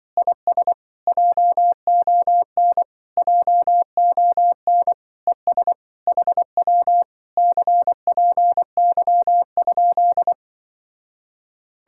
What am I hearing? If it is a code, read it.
Morse code, heard as a keyed tone, about 23 wpm, 700 Hz. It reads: IS JON JON ES HW CPY?
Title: Speed-Racing Common QSO Phrases - 20wpm